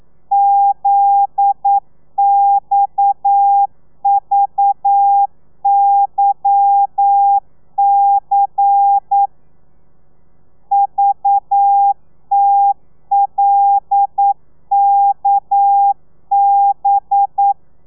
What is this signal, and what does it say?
Morse code, heard as a keyed tone, about 9 wpm, 800 Hz. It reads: ZXVYC VTLKB